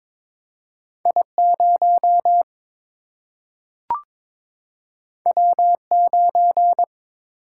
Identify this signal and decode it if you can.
Morse code, heard as a keyed tone, about 22 wpm, 700 Hz. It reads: I 0 W 9
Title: Single-Letter Single-Number 22wpm